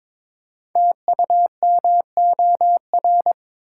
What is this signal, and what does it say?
Morse code, heard as a keyed tone, about 23 wpm, 700 Hz. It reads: TUMOR
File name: 5 Letter Words - First 3 Letters Given - 22wpm